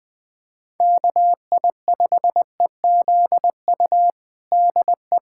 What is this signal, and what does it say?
Morse code, heard as a keyed tone, about 20 wpm, 700 Hz. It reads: KI5EZU DE
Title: USA Callsign DE Callsign - 6 Characters - 20wpm